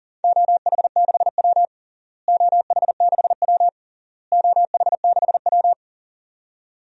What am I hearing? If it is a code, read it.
Morse code, heard as a keyed tone, about 37 wpm, 700 Hz. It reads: OH6W OH6W OH6W
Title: DXCC-100 Call Signs - 4 Characters - Repeated 3x - 40wpm